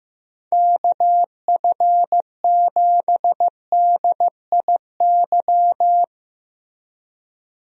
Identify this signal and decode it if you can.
Morse code, heard as a keyed tone, about 15 wpm, 700 Hz. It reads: KF7DIY
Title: USA Call Signs - 6 Characters - Repeated 2x - 15wpm